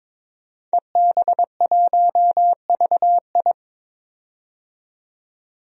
Morse code, heard as a keyed tone, about 21 wpm, 700 Hz. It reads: EB1VI
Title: DXCC-100 Call Signs - 5 Characters - 22wpm